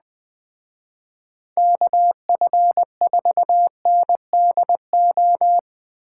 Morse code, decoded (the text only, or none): KF4NDO